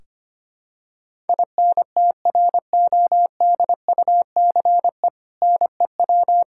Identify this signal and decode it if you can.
Morse code, heard as a keyed tone, about 25 wpm, 700 Hz. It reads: INTRODUCE NEW
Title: Sentences from Top 1600 Words 25wpm